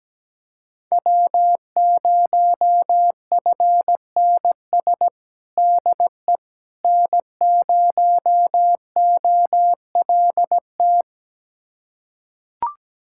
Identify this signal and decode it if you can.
Morse code, heard as a keyed tone, about 17 wpm, 700 Hz. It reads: W0FNS DE N0OLT
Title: USA Callsign DE Callsign - 5 Characters - 17wpm